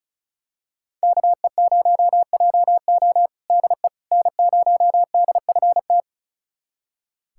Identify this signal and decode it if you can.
Morse code, heard as a keyed tone, about 36 wpm, 700 Hz. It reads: KE0JO DE N0DFT
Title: USA Callsign DE Callsign - 5 Characters - 35wpm